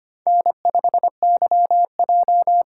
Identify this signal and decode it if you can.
Morse code, heard as a keyed tone, about 29 wpm, 700 Hz. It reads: N5YJ
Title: Speed Racing - USA 1x2 Call Signs 25wpm